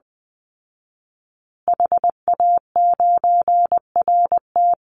Morse code, heard as a keyed tone, about 20 wpm, 700 Hz. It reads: HA9RT
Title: Call Signs Repeated 3x - 20wpm